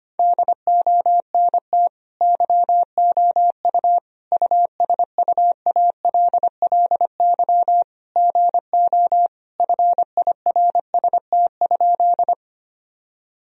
Morse code, heard as a keyed tone, about 25 wpm, 700 Hz. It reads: DONT YOU USUALLY GO FIRST?
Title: Sentences from Top 1000 Words - Review Part 3 - 25wpm